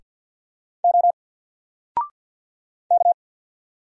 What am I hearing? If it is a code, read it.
Morse code, heard as a keyed tone, about 48 wpm, 700 Hz. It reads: O K
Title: Single Letter-Number Alt 50wpm